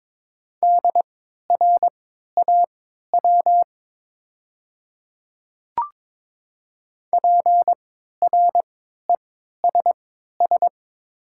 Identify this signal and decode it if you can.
Morse code, heard as a keyed tone, about 22 wpm, spaced as for 12 wpm, 700 Hz. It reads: DRAW PRESS
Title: 3x Character Spacing - Top 300 Words - 22wpm